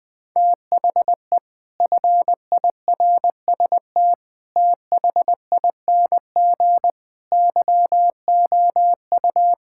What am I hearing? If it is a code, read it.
Morse code, heard as a keyed tone, about 20 wpm, 700 Hz. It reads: THE FIRST THING YOU
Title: Sets of 8 Words - 20wpm